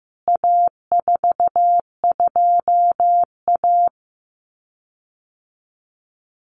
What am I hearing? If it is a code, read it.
Morse code, heard as a keyed tone, about 15 wpm, 700 Hz. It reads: A42A
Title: CWT Exchange 15wpm